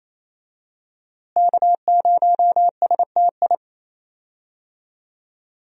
Morse code, heard as a keyed tone, about 28 wpm, 700 Hz. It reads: K0STI
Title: USA Call Signs - 5 Characters - Repeated 2x - 28wpm